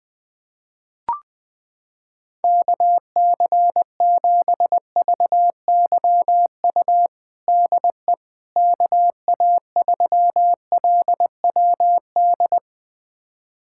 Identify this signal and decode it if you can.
Morse code, heard as a keyed tone, about 21 wpm, 700 Hz. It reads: KC7VYU DE KA3LWD
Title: USA Callsign DE Callsign 20wpm